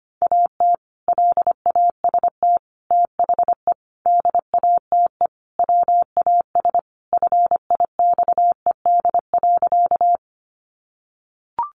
Morse code, heard as a keyed tone, about 25 wpm, 700 Hz. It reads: AT LAST THE DATE WAS FIXED.
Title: Sentences from Top 1100 Words 25wpm